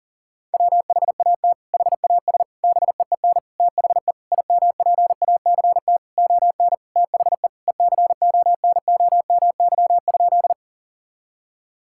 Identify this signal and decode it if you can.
Morse code, heard as a keyed tone, about 37 wpm, 700 Hz. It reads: WHAT HAS BEEN THE IMPACT ON THE ECONOMY?
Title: Sentences from Top 1000 Words - Review Part 6 - 40wpm